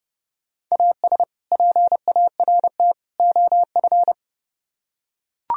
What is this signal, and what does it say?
Morse code, heard as a keyed tone, about 30 wpm, 700 Hz. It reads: AS PART OF